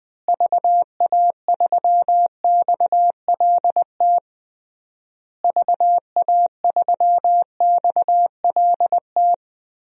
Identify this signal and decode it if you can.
Morse code, heard as a keyed tone, about 21 wpm, 700 Hz. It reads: VA3XLT VA3XLT
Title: DXCC-100 Call Signs - 6 Characters - Repeated 3x - 20wpm